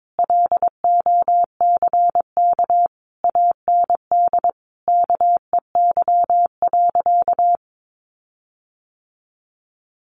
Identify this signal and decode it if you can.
Morse code, heard as a keyed tone, about 22 wpm, 700 Hz. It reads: LOCK AND KEY.
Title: Sentences from Top 1400 Words 22wpm